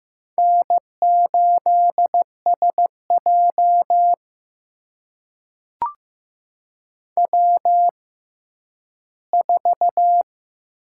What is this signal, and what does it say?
Morse code, heard as a keyed tone, about 15 wpm, spaced as for 10 wpm, 700 Hz. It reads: N8SJ W4
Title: USA Call Signs - 4 Characters - 6x Character Spacing - 15wpm